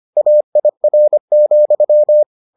Morse code, heard as a keyed tone, about 25 wpm, 600 Hz. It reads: AIR,